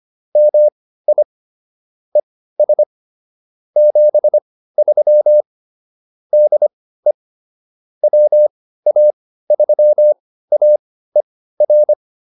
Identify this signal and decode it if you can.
Morse code, heard as a keyed tone, about 25 wpm, spaced as for 15 wpm, 600 Hz. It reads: MI ES 73 DE WA3AER